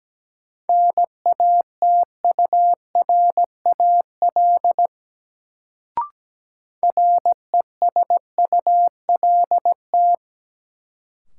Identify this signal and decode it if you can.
Morse code, heard as a keyed tone, about 17 wpm, 700 Hz. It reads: NATURAL RESULT
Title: Sentences from Top 700 Words - New Word Drill - 17wpm